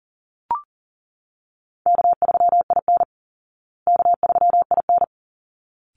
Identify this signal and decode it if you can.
Morse code, heard as a keyed tone, about 40 wpm, 700 Hz. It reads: K3IN K3IN
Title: Call Signs Repeated 3x - 40wpm